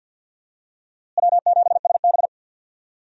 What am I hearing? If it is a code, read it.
Morse code, heard as a keyed tone, about 52 wpm, 700 Hz. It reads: W7SB
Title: USA Call Signs - 4 Characters - Repeated 2x - 50wpm